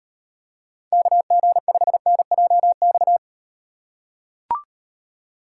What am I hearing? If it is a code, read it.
Morse code, heard as a keyed tone, about 38 wpm, 700 Hz. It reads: KG5NJX